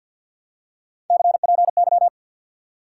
Morse code, heard as a keyed tone, about 50 wpm, 700 Hz. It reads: CPY
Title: Top 90 QSO Elements 50wpm